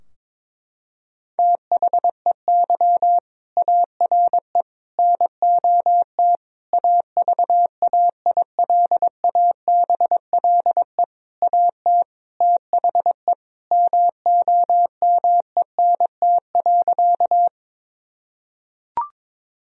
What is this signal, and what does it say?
Morse code, heard as a keyed tone, about 22 wpm, 700 Hz. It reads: THEY ARE NOT AVAILABLE AT THE MOMENT.
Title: Sentences from Top 1000 Words - Review Part 3 - 22wpm